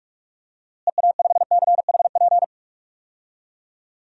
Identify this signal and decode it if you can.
Morse code, heard as a keyed tone, about 41 wpm, 700 Hz. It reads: EA5CHP
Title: DXCC-100 Call Signs - 6 Characters - Repeated 2x - 45wpm